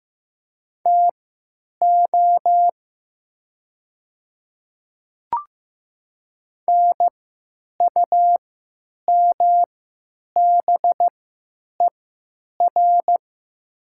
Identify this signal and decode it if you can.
Morse code, heard as a keyed tone, about 15 wpm, spaced as for 9 wpm, 700 Hz. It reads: TO NUMBER
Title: 3x Character Spacing - Top 100 Words - 15wpm